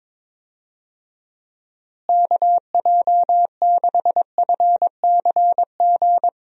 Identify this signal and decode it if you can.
Morse code, heard as a keyed tone, about 22 wpm, 700 Hz. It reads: KJ6FCG
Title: USA Call Signs - 6 Characters - Repeated 3x - 22wpm